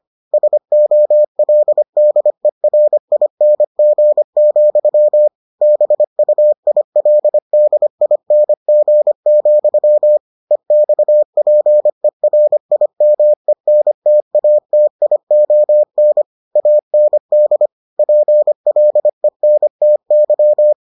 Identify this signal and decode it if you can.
Morse code, heard as a keyed tone, about 25 wpm, 600 Hz. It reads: SOLDERING, BUILDING, EXPERIMENTATION AND PLENTY